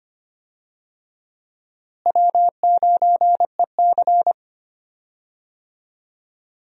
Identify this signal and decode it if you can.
Morse code, heard as a keyed tone, about 25 wpm, 700 Hz. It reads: W9EC
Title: USA Call Signs - 4 Characters - Repeated 2x - 25wpm